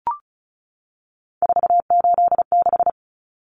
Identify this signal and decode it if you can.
Morse code, heard as a keyed tone, about 34 wpm, 700 Hz. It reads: 486
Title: Common Cut Numbers - 3-Digit 35wpm